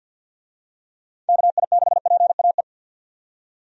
Morse code, heard as a keyed tone, about 50 wpm, 700 Hz. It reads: KI6PAE
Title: USA Call Signs - 6 Characters - 50wpm